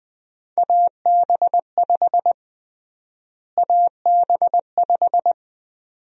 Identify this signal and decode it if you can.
Morse code, heard as a keyed tone, about 20 wpm, 700 Hz. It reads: AB5 AB5
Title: USA Call Sign Prefixes - Repeated 2x - 20wpm